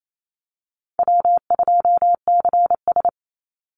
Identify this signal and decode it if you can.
Morse code, heard as a keyed tone, about 28 wpm, 700 Hz. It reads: W2CS
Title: Call Signs Repeated 3x - 28wpm